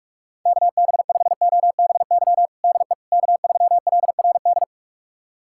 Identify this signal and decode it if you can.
Morse code, heard as a keyed tone, about 46 wpm, 700 Hz. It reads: KB5OBY DE K3LRD
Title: USA Callsign DE Callsign 45wpm